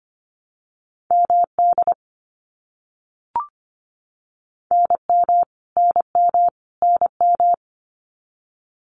Morse code, heard as a keyed tone, about 25 wpm, 700 Hz. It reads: MD NM NM NM
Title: US States Repeated 3x 25wpm